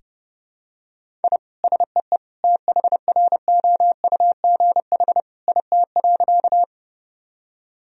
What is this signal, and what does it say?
Morse code, heard as a keyed tone, about 32 wpm, 700 Hz. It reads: I SEE THROUGH IT.
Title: Sentences from Top 1000 Words - Review Part 1 - 30wpm